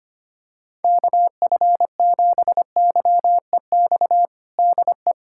Morse code, heard as a keyed tone, about 25 wpm, 700 Hz. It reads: KF7YEX DE
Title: USA Callsign DE Callsign 25wpm